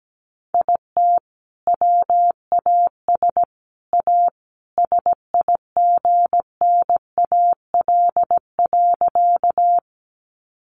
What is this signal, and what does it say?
Morse code, heard as a keyed tone, about 17 wpm, 700 Hz. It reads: IT WAS A SIGNAL.
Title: Sentences from Top 1400 Words 17wpm